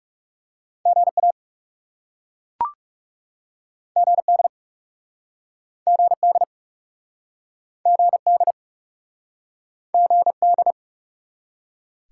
Morse code, heard as a keyed tone, about 40 wpm, 700 Hz. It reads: GA GD GD GD GD